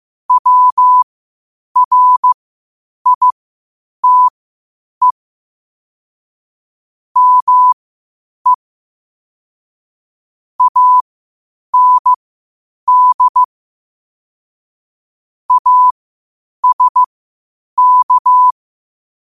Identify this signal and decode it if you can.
Morse code, heard as a keyed tone, about 15 wpm, spaced as for 8 wpm, 1000 Hz. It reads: WRITE ME AND ASK
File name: qotdfiles / 8wpm